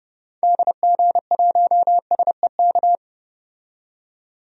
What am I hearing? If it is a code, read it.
Morse code, heard as a keyed tone, about 30 wpm, 700 Hz. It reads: DG1SEK
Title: DXCC-100 Call Signs - 6 Characters - 30wpm